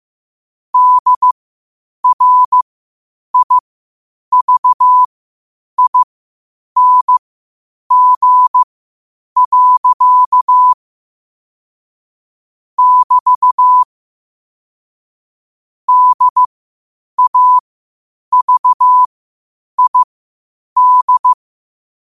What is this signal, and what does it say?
Morse code, heard as a keyed tone, about 15 wpm, spaced as for 8 wpm, 1000 Hz. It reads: DRIVING. <BT> DAVID
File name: qotdfiles / 8wpm